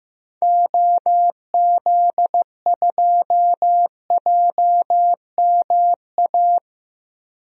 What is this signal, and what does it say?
Morse code, heard as a keyed tone, about 15 wpm, 700 Hz. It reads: OZ2JMA
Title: DXCC-100 Call Signs - 6 Characters - Repeated 3x - 15wpm